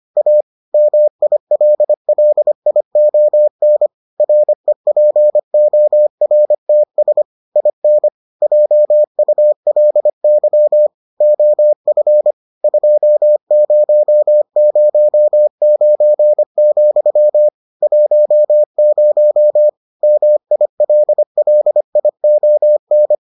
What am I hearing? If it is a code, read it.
Morse code, heard as a keyed tone, about 25 wpm, 600 Hz. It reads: A MILLION REPORTS IN JULY OF 2009, 10 MILLION